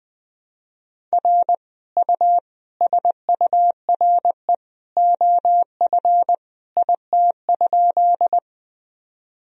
Morse code, heard as a keyed tone, about 21 wpm, 700 Hz. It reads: R U SURE OF IT?
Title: Sentences from Top 300 Words and Abbreviations 20wpm